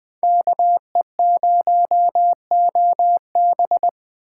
Morse code, heard as a keyed tone, about 20 wpm, 700 Hz. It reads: KE0OB
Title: Speed Racing - USA 2x2 Call Signs 17wpm